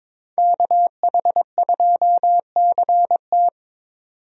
Morse code, heard as a keyed tone, about 22 wpm, 700 Hz. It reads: KH2CT
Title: USA Call Signs - 5 Characters - 22wpm